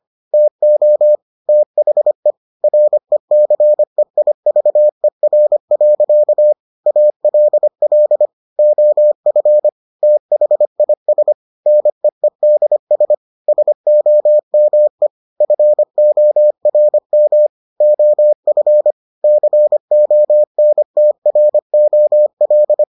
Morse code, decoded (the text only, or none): TO THE RECEIVER. ALL OF THIS NEEDS SOME FORM OF CONTROL